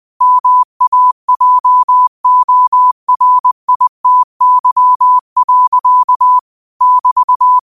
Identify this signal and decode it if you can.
Morse code, heard as a keyed tone, about 20 wpm, 1000 Hz. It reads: MAJORITY. <BT>
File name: 20wpm